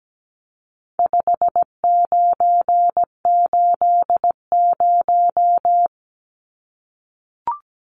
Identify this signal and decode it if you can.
Morse code, heard as a keyed tone, about 17 wpm, 700 Hz. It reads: 5980
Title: Numbers - 4-Digit 17wpm